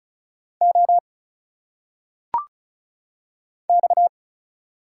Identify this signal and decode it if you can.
Morse code, heard as a keyed tone, about 34 wpm, 700 Hz. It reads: O X
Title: Single Letters - Phonetic Pronunciation 35wpm